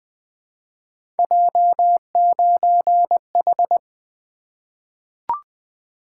Morse code, heard as a keyed tone, about 20 wpm, 700 Hz. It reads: J9H
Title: Three Letter-Number Combinations - 20wpm